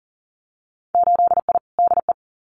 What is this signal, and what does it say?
Morse code, heard as a keyed tone, about 40 wpm, 700 Hz. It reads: 8I DE
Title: Winter Field Day Exchanges Encore 40wpm